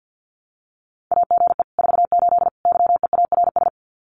Morse code, heard as a keyed tone, about 48 wpm, 700 Hz. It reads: AGE 48 YEARS
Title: Common QSO Phrases Encore 50wpm